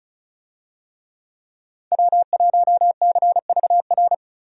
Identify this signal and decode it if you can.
Morse code, heard as a keyed tone, about 36 wpm, 700 Hz. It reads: W1CVR
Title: USA Call Signs - 5 Characters - 35wpm